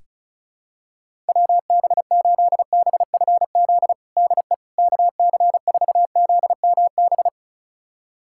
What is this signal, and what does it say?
Morse code, heard as a keyed tone, about 36 wpm, 700 Hz. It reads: WB8BFZ DE KC4ZMB
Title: USA Callsign DE Callsign - 6 Characters - 35wpm